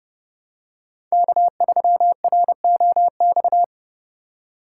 Morse code, heard as a keyed tone, about 30 wpm, 700 Hz. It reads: K3ROX
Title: USA Call Signs - 5 Characters - 4x Character Spacing - 30wpm